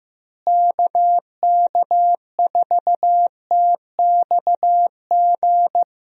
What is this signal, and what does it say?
Morse code, heard as a keyed tone, about 15 wpm, 700 Hz. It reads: KK4TXG
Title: USA Call Signs - 6 Characters - Repeated 2x - 15wpm